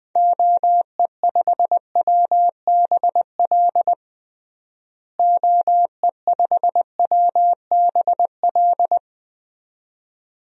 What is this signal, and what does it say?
Morse code, heard as a keyed tone, about 21 wpm, 700 Hz. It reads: OE5WBL OE5WBL
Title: DXCC-100 Call Signs - 6 Characters - Repeated 2x - 20wpm